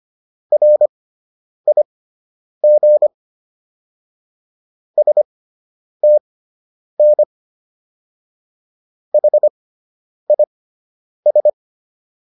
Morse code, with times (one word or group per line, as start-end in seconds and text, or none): 0.5-3.1 RIG
5.0-7.2 STN
9.1-11.5 HIS